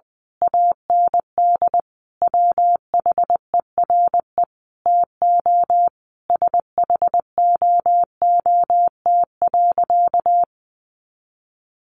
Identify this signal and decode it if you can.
Morse code, heard as a keyed tone, about 20 wpm, 700 Hz. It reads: AND WHERE TO SHOOT.
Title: A Sound Of Thunder 20wpm